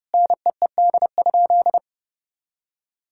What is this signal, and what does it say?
Morse code, heard as a keyed tone, about 32 wpm, 700 Hz. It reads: NEED?
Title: Sentences from Top 1000 Words - Review Part 5 - 30wpm